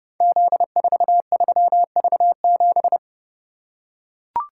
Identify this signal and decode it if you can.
Morse code, heard as a keyed tone, about 30 wpm, 700 Hz. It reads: Z43V7
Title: Code Groups - 5 Letter:Numbers - 30wpm